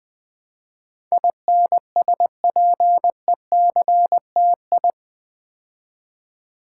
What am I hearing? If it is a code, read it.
Morse code, heard as a keyed tone, about 21 wpm, 700 Hz. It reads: INSPECTI
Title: Build A Word - 4 Letters At A Time - 10-Letter Words - 20wpm